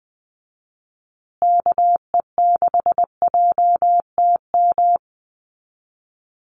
Morse code, heard as a keyed tone, about 20 wpm, 700 Hz. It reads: KE6JTM
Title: USA Call Signs Repeated 3x 20wpm